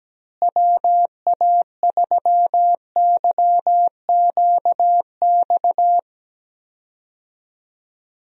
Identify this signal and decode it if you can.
Morse code, heard as a keyed tone, about 17 wpm, 700 Hz. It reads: WA3YQX
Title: USA Call Signs - 6 Characters - Repeated 2x - 17wpm